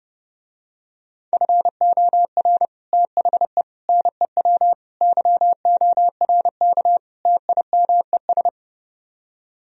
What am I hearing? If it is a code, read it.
Morse code, heard as a keyed tone, about 29 wpm, 700 Hz. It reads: FOR THE NEW YORK TIMES